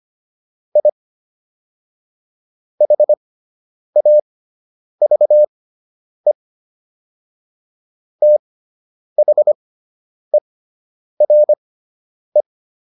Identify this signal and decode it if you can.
Morse code, heard as a keyed tone, about 25 wpm, spaced as for 9 wpm, 600 Hz. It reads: I HAVE THERE